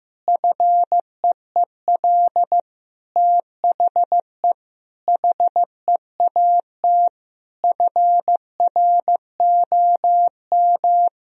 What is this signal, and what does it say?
Morse code, heard as a keyed tone, about 15 wpm, 700 Hz. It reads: FEEL THE HEAT FROM
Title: Sentences from Top 1000 Words - Review Part 3 - 15wpm